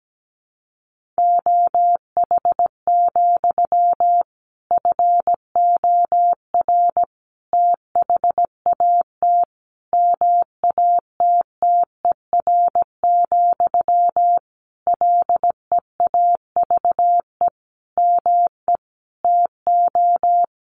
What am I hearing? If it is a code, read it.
Morse code, heard as a keyed tone, about 17 wpm, 700 Hz. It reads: OH, FOR THAT MATTER, LEAVE ME TO
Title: Sentences from Top 1400 Words 17wpm